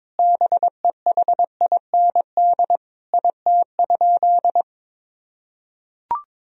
Morse code, heard as a keyed tone, about 22 wpm, 700 Hz. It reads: BEHIND IT?
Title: Sentences from Top 1000 Words - Review Part 4 - 22wpm